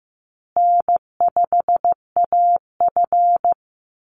Morse code, heard as a keyed tone, about 15 wpm, 700 Hz. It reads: N5AF
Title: Easy Call Signs Repeated 3x - 15wpm